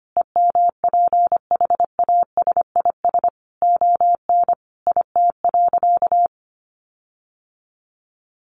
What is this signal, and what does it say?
Morse code, heard as a keyed tone, about 25 wpm, 700 Hz. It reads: EMPHASIS ON IT.
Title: Sentences from Top 1400 Words 25wpm